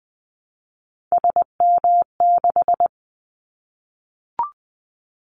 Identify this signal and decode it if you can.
Morse code, heard as a keyed tone, about 20 wpm, 700 Hz. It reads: SM6
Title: Callsign Prefixes for Top 25 DXCC - Repeated 3x - 20wpm